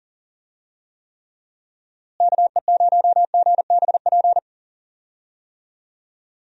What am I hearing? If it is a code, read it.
Morse code, heard as a keyed tone, about 40 wpm, 700 Hz. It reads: KE0GBP
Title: USA Call Signs - 6 Characters - Repeated 3x - 40wpm